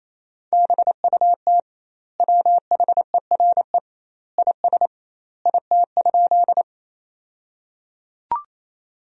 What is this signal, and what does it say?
Morse code, heard as a keyed tone, about 28 wpm, 700 Hz. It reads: BUT WHERE IS IT?
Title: 2x Word-Spacing - Sentences from Top 200 Words - 28wpm